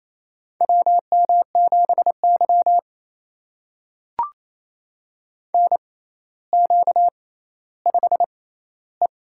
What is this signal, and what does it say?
Morse code, heard as a keyed tone, about 28 wpm, spaced as for 14 wpm, 700 Hz. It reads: WM7Y NQ5E
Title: USA Call Signs - 4 Characters - 6x Character Spacing - 28wpm